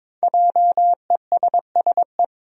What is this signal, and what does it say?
Morse code, heard as a keyed tone, about 22 wpm, 700 Hz. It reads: JESSE
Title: USA Common Names - 22wpm